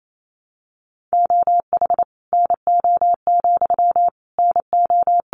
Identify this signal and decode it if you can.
Morse code, heard as a keyed tone, about 28 wpm, 700 Hz. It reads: OH NO, NO
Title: Sentences from Top 1300 Words 28wpm